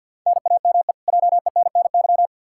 Morse code, heard as a keyed tone, about 52 wpm, 700 Hz. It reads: NAME JENNY
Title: K1USN SST QSO Practice 50wpm